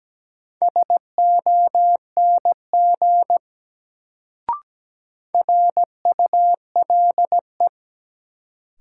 Morse code, heard as a keyed tone, about 17 wpm, 700 Hz. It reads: SONG RULE
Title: Sentences from Top 400 Words - New Word Drill - 17wpm